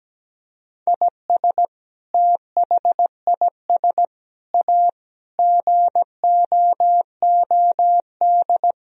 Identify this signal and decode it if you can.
Morse code, heard as a keyed tone, about 17 wpm, 700 Hz. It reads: IS THIS A GOOD
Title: Sentences from Top 1000 Words - Review Part 4 - 17wpm